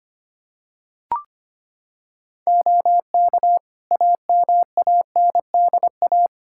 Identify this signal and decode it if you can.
Morse code, heard as a keyed tone, about 25 wpm, 700 Hz. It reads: OK AMANDA
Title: K1USN SST QSO Practice 25wpm